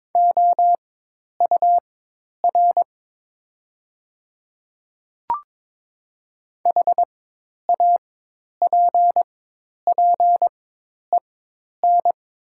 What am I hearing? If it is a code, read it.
Morse code, heard as a keyed tone, about 22 wpm, spaced as for 10 wpm, 700 Hz. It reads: OUR HAPPEN
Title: 4x Character Spacing - Top 500 Words - 22wpm